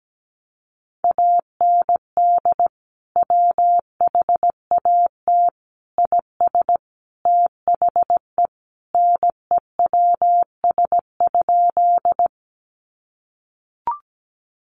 Morse code, heard as a keyed tone, about 17 wpm, 700 Hz. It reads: AND WHAT IS THE NEWS?
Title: Sentences from Top 800 Words 17wpm